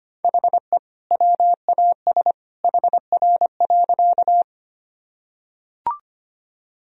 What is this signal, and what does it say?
Morse code, heard as a keyed tone, about 25 wpm, 700 Hz. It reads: HE WAS HR.
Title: Sentences from Top 200 Words and Abbreviations 25wpm